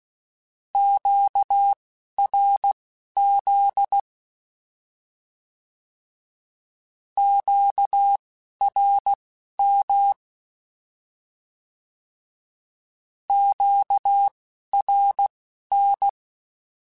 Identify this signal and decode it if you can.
Morse code, heard as a keyed tone, about 16 wpm, spaced as for 11 wpm, 800 Hz. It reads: QRZ QRM QRN
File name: Short Runs / Short Runs-mp3s / Words @10